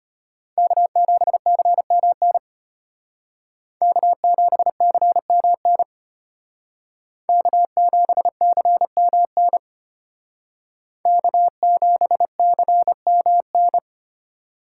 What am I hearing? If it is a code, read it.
Morse code, heard as a keyed tone, about 32 wpm, 700 Hz. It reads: K7CMN K7CMN K7CMN K7CMN